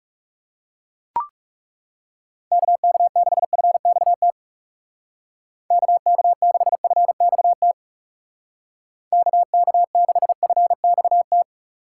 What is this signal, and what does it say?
Morse code, heard as a keyed tone, about 36 wpm, 700 Hz. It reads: KK6FXT KK6FXT KK6FXT
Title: Speed Racing - USA Call Signs 30wpm